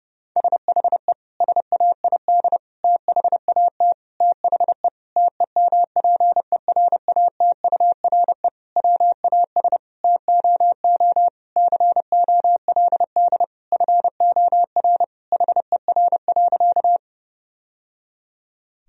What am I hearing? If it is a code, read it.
Morse code, heard as a keyed tone, about 30 wpm, 700 Hz. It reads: SHE SAID THAT THE TEMPERATURE WAS TOO COLD FOR HER.
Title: Sentences from Top 1000 Words - Review Part 4 - 30wpm